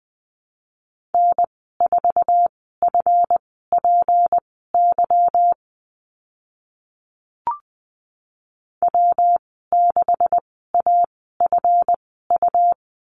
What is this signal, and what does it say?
Morse code, heard as a keyed tone, about 20 wpm, spaced as for 15 wpm, 700 Hz. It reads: N4FPY W6AFU
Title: 186 Lesson 30g - Letter b - Call Signs 20wpm